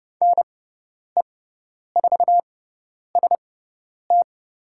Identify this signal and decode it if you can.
Morse code, heard as a keyed tone, about 30 wpm, spaced as for 10 wpm, 700 Hz. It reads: NE4ST
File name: Lesson 18g - Letter m - Call Signs 30wpm